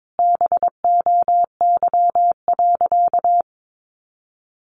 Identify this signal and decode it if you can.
Morse code, heard as a keyed tone, about 22 wpm, 700 Hz. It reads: BOY.
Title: Sentences from Top 200 Words 22wpm